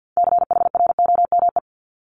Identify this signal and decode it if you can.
Morse code, heard as a keyed tone, about 50 wpm, 700 Hz. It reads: CHROME
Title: Multi-letter Phonemes Repeated 3x 50wpm